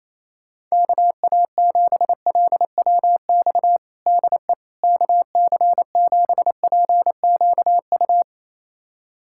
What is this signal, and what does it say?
Morse code, heard as a keyed tone, about 28 wpm, 700 Hz. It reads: KA7LWX DE KC7PQU
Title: USA Callsign DE Callsign - 6 Characters - 28wpm